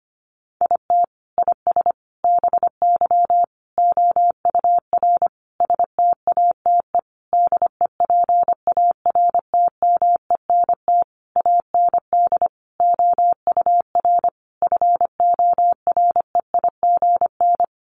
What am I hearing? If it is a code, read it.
Morse code, heard as a keyed tone, about 25 wpm, 700 Hz. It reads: IT IS BY OUR STATE DEPARTMENT AND OUR FOREIGN
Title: Sentences from Top 1400 Words 25wpm